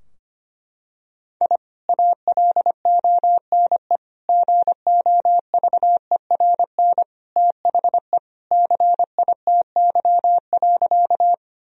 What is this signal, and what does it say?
Morse code, heard as a keyed tone, about 25 wpm, 700 Hz. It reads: I ALONE GOVERN THE CITY.
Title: Sentences from Top 1000 Words - Review Part 2 - 25wpm